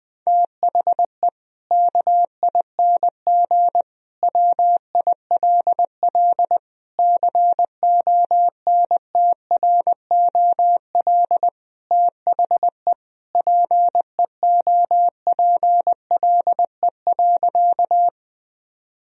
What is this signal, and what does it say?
Morse code, heard as a keyed tone, about 20 wpm, 700 Hz. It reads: THE KING WILL CONTROL THE PEOPLE.
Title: Sentences from Top 1000 Words - Review Part 3 - 20wpm